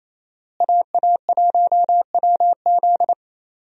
Morse code, heard as a keyed tone, about 28 wpm, 700 Hz. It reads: AA1WZ
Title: USA Call Signs - 5 Characters - Repeated 2x - 28wpm